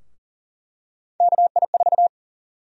Morse code, heard as a keyed tone, about 40 wpm, 700 Hz. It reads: KI4
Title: USA Call Sign Prefixes - Repeated 2x - 40wpm